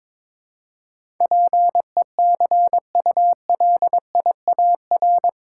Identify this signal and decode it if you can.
Morse code, heard as a keyed tone, about 21 wpm, 700 Hz. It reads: PECULIAR
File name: practice / 22wpm / Eight Letter Words - 22wpm